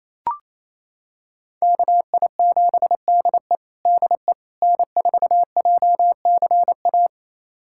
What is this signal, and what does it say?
Morse code, heard as a keyed tone, about 28 wpm, 690 Hz. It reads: KI7DE DE N4JCA
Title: USA Callsign DE Callsign - 5 Characters - 28wpm